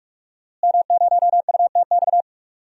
Morse code, heard as a keyed tone, about 46 wpm, 700 Hz. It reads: M0UTX
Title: DXCC-100 Call Signs - 5 Characters - Repeated 2x - 45wpm